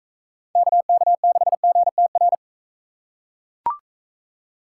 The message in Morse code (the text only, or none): KK6GTR